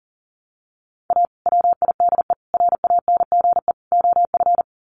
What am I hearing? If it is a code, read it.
Morse code, heard as a keyed tone, about 40 wpm, 700 Hz. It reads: A WIDE RANGE OF
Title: Sets of 4 Words 40wpm